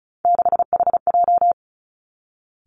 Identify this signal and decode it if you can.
Morse code, heard as a keyed tone, about 36 wpm, 700 Hz. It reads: 6HJ
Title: Three Challenging Letter-Number Combinations 35wpm